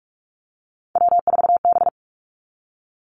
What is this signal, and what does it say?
Morse code, heard as a keyed tone, about 46 wpm, 700 Hz. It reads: W4B
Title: Three Challenging Letter-Number Combinations 45wpm